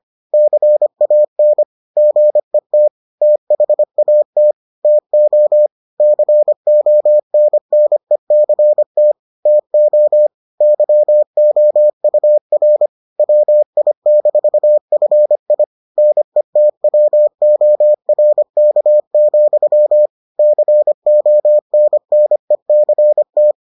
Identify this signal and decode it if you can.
Morse code, heard as a keyed tone, about 25 wpm, 600 Hz. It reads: CAN GET THAT TO CONNECT TO YOUR WI-FI NETWORK, CONNECT